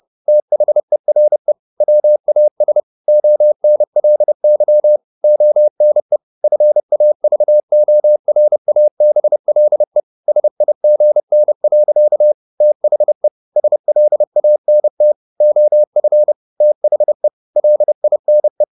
Morse code, heard as a keyed tone, about 30 wpm, 600 Hz. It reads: THERE WAS ONLY ONE FAVORABLE SIGN. THE SLANT OF THE LINE
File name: the old man and the sea / the old man and the sea 1 30wpm